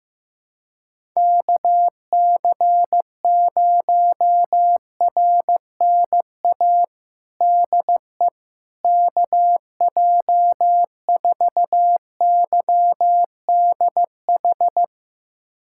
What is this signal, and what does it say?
Morse code, heard as a keyed tone, about 15 wpm, 700 Hz. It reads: KC0RNA DE KJ4YDH
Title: USA Callsign DE Callsign 15wpm